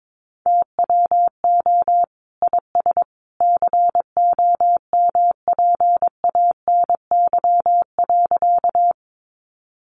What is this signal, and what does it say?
Morse code, heard as a keyed tone, about 22 wpm, 700 Hz. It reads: TWO IS COMPANY.
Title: Sentences from Top 600 Words 22wpm